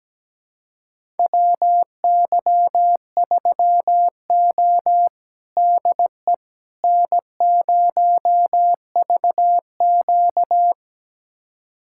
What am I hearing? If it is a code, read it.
Morse code, heard as a keyed tone, about 17 wpm, 700 Hz. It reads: WY3O DE N0VQ
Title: USA Callsign DE Callsign - 4 Characters - 17wpm